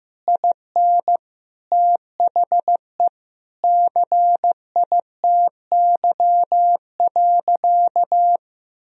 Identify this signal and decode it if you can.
Morse code, heard as a keyed tone, about 14 wpm, 700 Hz. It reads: IN THE CITY.
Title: Sentences from Top 1000 Words - Review Part 4 - 15wpm